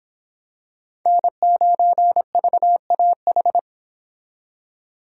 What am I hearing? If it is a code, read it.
Morse code, heard as a keyed tone, about 26 wpm, 700 Hz. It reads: N9VAH